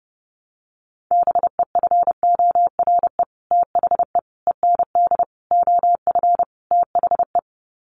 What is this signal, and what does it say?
Morse code, heard as a keyed tone, about 30 wpm, 700 Hz. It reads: BEFORE THE END OF THE
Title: Sets of 5 Words Encore 30wpm